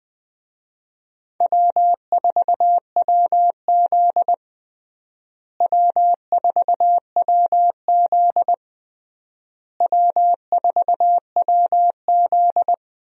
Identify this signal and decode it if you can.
Morse code, heard as a keyed tone, about 20 wpm, 700 Hz. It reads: W4WZ W4WZ W4WZ
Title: USA Call Signs - 4 Characters - Repeated 3x - 20wpm